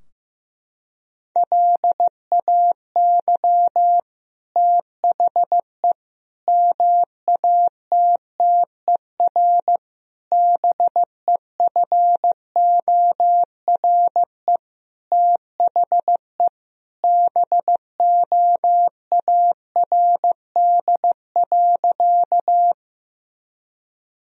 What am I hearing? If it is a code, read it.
Morse code, heard as a keyed tone, about 15 wpm, 700 Hz. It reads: LAY THE MATTER BEFORE THE BOARD.
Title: Sentences from Top 1000 Words - Review Part 4 - 15wpm